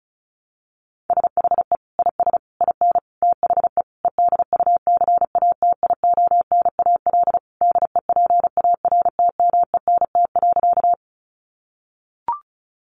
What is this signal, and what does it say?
Morse code, heard as a keyed tone, about 34 wpm, 700 Hz. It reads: SHE IS IN THE EDUCATIONAL DEPARTMENT.
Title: Sentences from Top 1200 Words 35wpm